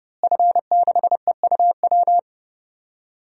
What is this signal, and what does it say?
Morse code, heard as a keyed tone, about 31 wpm, 700 Hz. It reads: F6EUW
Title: DXCC-100 Call Signs - 5 Characters - Repeated 2x - 30wpm